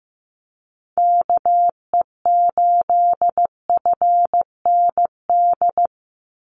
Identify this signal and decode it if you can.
Morse code, heard as a keyed tone, about 15 wpm, 700 Hz. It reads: KE8FND
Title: USA Call Signs 15wpm